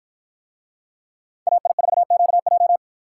Morse code, heard as a keyed tone, about 53 wpm, 700 Hz. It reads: AI4QJ